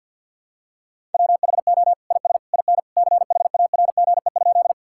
Morse code, heard as a keyed tone, about 51 wpm, 700 Hz. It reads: WHO IS IN CHARGE?
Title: Sentences from Top 1000 Words - Review Part 6 - 50wpm